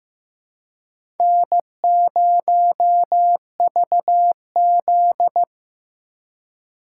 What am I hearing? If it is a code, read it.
Morse code, heard as a keyed tone, about 18 wpm, 700 Hz. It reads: N0VZ
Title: Speed Racing - USA Call Signs 15wpm